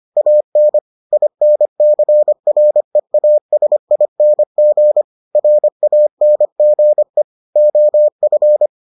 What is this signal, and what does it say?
Morse code, heard as a keyed tone, about 25 wpm, 600 Hz. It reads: AN INCREASING RANGE OF